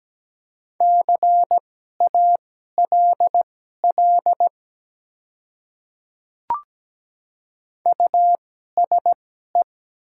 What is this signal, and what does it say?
Morse code, heard as a keyed tone, about 17 wpm, spaced as for 12 wpm, 700 Hz. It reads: CALL USE
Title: 2x Character Spacing - Top 100 Words - 17wpm